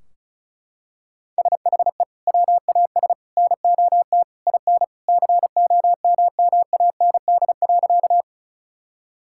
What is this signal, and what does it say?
Morse code, heard as a keyed tone, about 36 wpm, 700 Hz. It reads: SHE WAS NOT IN COMMAND.
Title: Sentences from Top 1600 Words 35wpm